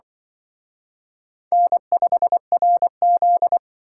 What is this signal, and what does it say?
Morse code, heard as a keyed tone, about 24 wpm, 700 Hz. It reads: N5RZ
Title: Speed Racing - USA 1x2 Call Signs 20wpm